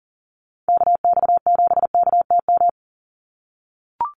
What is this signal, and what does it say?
Morse code, heard as a keyed tone, about 40 wpm, 700 Hz. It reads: KX7KTM
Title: USA Call Signs 40wpm